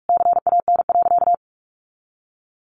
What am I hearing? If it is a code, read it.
Morse code, heard as a keyed tone, about 44 wpm, 700 Hz. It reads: CAN.
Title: Additional Easy Phrases 45wpm